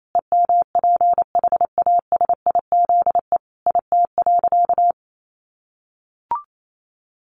Morse code, heard as a keyed tone, about 28 wpm, 700 Hz. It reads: EMPHASIZE IT.
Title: Sentences from Top 1500 Words 28wpm